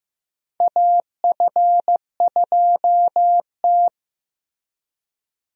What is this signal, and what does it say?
Morse code, heard as a keyed tone, about 15 wpm, 700 Hz. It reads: AF2T
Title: USA Call Signs - 4 Characters - Repeated 3x - 15wpm